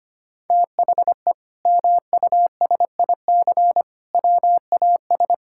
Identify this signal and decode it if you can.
Morse code, heard as a keyed tone, about 25 wpm, 700 Hz. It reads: THE MUSIC WAS
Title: Sentences from Top 1000 Words - Review Part 6 - 25wpm